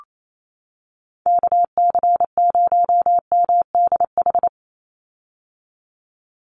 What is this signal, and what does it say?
Morse code, heard as a keyed tone, about 28 wpm, 700 Hz. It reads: KC0MDH
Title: USA Call Signs 28wpm